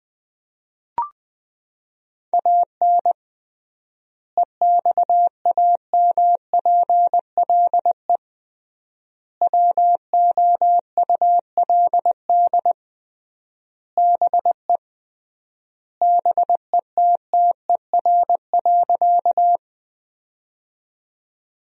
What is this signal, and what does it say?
Morse code, heard as a keyed tone, about 21 wpm, 700 Hz. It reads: AN EXAMPLE WOULD BE BETTER.
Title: 3x Word-Spacing - Sentences from Top 400 Words - 20wpm